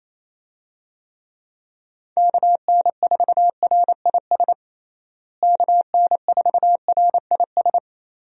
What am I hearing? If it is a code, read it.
Morse code, heard as a keyed tone, about 28 wpm, 700 Hz. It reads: KN4RIS KN4RIS
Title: USA Call Signs - 6 Characters - Repeated 3x - 28wpm